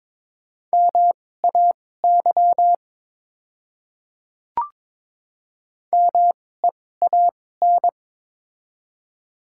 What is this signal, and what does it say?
Morse code, heard as a keyed tone, about 22 wpm, spaced as for 16 wpm, 700 Hz. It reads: MAY MEAN
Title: 2x Character Spacing - Top 300 Words - 22wpm